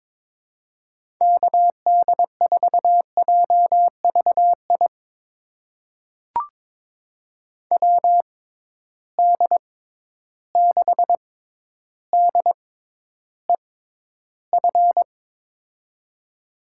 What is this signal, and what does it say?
Morse code, heard as a keyed tone, about 22 wpm, spaced as for 11 wpm, 700 Hz. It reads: KD4JVI WD6DEF
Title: USA Call Signs - 6 Characters - 6x Character Spacing - 22wpm